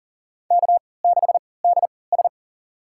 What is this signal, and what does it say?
Morse code, heard as a keyed tone, about 40 wpm, spaced as for 23 wpm, 700 Hz. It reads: K6DS
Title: USA Call Signs - 4 Characters - 3x Character Spacing - 40wpm